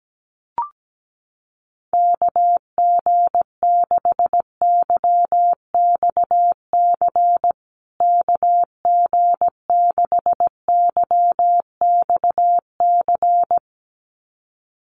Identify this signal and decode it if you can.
Morse code, heard as a keyed tone, about 17 wpm, 700 Hz. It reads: KG6YXC KG6YXC
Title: USA Call Signs Repeated 2x 17wpm